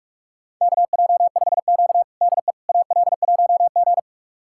Morse code, heard as a keyed tone, about 46 wpm, 700 Hz. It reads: KJ5Q DE AL1G
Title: USA Callsign DE Callsign - 4 Characters - 45wpm